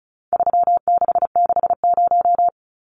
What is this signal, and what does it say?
Morse code, heard as a keyed tone, about 34 wpm, 700 Hz. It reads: 3660
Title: Numbers - 4-Digit 35wpm